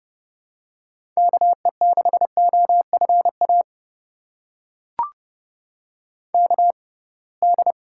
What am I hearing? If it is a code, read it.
Morse code, heard as a keyed tone, about 30 wpm, spaced as for 23 wpm, 700 Hz. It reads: KE6OFA KD
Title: USA Call Signs - 6 Characters - 6x Character Spacing - 30wpm